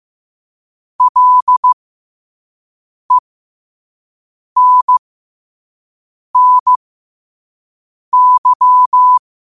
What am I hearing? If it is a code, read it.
Morse code, heard as a keyed tone, about 15 wpm, spaced as for 5 wpm, 1000 Hz. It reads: LENNY
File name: qotdfiles / 5wpm